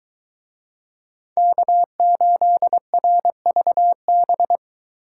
Morse code, heard as a keyed tone, about 22 wpm, 700 Hz. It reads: K8RVB